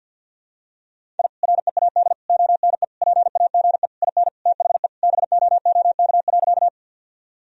Morse code, heard as a keyed tone, about 47 wpm, 700 Hz. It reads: I READ ONE PAGE IN THE BOOK.
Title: Sentences from Top 1000 Words - Review Part 1 - 50wpm